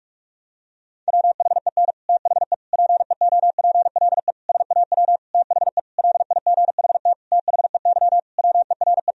Morse code, heard as a keyed tone, about 46 wpm, 700 Hz. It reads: WHEN THE PEOPLE SAW THE LIGHT THEY WERE
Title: Sentences from Top 1000 Words - Review Part 1 - 45wpm